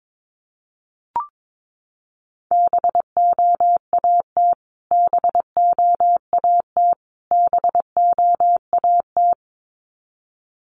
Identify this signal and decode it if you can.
Morse code, heard as a keyed tone, about 22 wpm, 700 Hz. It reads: BOAT BOAT BOAT
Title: Top 500 Words Repeated 22wpm